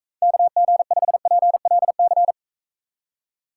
Morse code, heard as a keyed tone, about 42 wpm, 700 Hz. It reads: KG5PLC